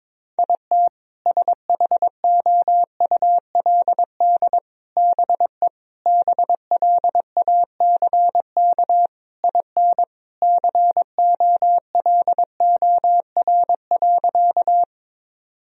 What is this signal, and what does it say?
Morse code, heard as a keyed tone, about 22 wpm, 700 Hz. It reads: IT SHOULD BE BLACK IN COLOR.
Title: Sentences from Top 1000 Words - Review Part 2 - 22wpm